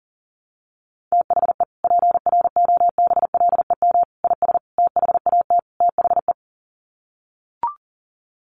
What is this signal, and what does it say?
Morse code, heard as a keyed tone, about 40 wpm, 700 Hz. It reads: THE PROBLEM IS THAT THE
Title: Sets of 5 Words Encore2 40wpm